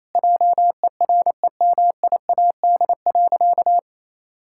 Morse code, heard as a keyed tone, about 28 wpm, 700 Hz. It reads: JEREMIAD.
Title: Pangram Challenge v2 - 28wpm